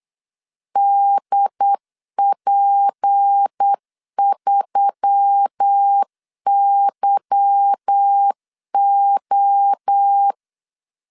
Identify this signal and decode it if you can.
Morse code, heard as a keyed tone, about 8 wpm, 800 Hz. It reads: DP3YO
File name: LEZIONI / 8 YJ45 / 8 Y 8wpm 3245